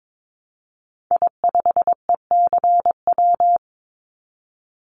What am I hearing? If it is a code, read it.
Morse code, heard as a keyed tone, about 22 wpm, 700 Hz. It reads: I5ECW
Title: Call Signs - 22wpm